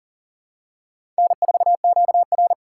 Morse code, heard as a keyed tone, about 40 wpm, 700 Hz. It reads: N4QR
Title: USA Call Signs - 4 Characters - 4x Character Spacing - 40wpm